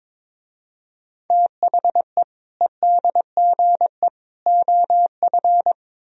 Morse code, heard as a keyed tone, about 24 wpm, 700 Hz. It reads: THE EDGE OF
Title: Speed-Racing - Sets of 3 Words 22wpm